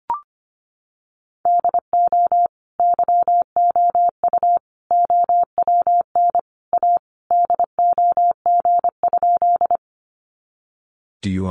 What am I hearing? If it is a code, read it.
Morse code, heard as a keyed tone, about 25 wpm, 700 Hz. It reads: DO YOU OWN A DOG?
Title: Sentences from Top 400 Words 25wpm